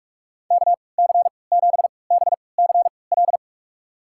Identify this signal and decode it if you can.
Morse code, heard as a keyed tone, about 46 wpm, spaced as for 26 wpm, 700 Hz. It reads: KC7BCL
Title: USA Call Signs - 3x Character Spacing - 45wpm